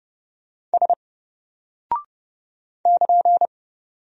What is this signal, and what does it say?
Morse code, heard as a keyed tone, about 29 wpm, 700 Hz. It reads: S <KN>
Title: Single Character (Letter:Number:Punctuation:Prosign) 30wpm